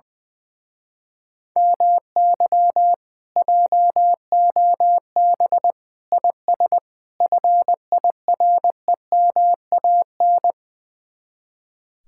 Morse code, heard as a keyed tone, about 20 wpm, 700 Hz. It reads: MY JOB IS FIREMAN
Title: Speed-Racing Common QSO Phrases - 20wpm